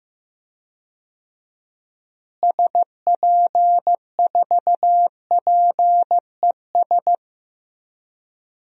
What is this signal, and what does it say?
Morse code, heard as a keyed tone, about 15 wpm, 700 Hz. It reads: SP4PES
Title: DXCC-100 Call Signs - 6 Characters - Repeated 2x - 15wpm